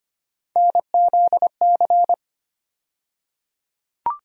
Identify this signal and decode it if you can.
Morse code, heard as a keyed tone, about 25 wpm, 700 Hz. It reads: NZC
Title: USA Call Sign Postfixes - 3 Letters 25wpm